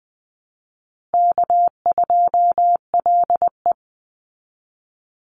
Morse code, heard as a keyed tone, about 20 wpm, 700 Hz. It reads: K2LE
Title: Easy Call Signs Repeated 3x - 20wpm